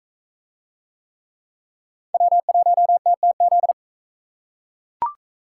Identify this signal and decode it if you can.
Morse code, heard as a keyed tone, about 41 wpm, 700 Hz. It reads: W1TTZ